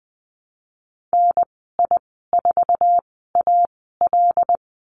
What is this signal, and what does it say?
Morse code, heard as a keyed tone, about 20 wpm, spaced as for 15 wpm, 700 Hz. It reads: NI4AL
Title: 088 Lesson 15g - Number 5 - Call Signs 20wpm